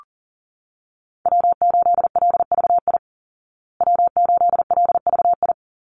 Morse code, heard as a keyed tone, about 40 wpm, 700 Hz. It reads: W8LVI W8LVI
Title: Call Signs Repeated 2x - 40wpm